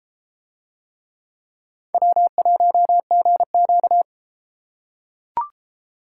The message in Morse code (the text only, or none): W1GQ